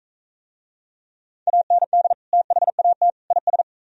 Morse code, heard as a keyed tone, about 42 wpm, 700 Hz. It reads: AND THAT IS